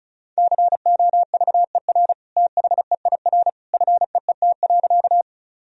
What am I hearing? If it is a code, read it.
Morse code, heard as a keyed tone, about 36 wpm, 700 Hz. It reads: COVER THEIR FEET.
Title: Sentences from Top 1000 Words - Review Part 1 - 35wpm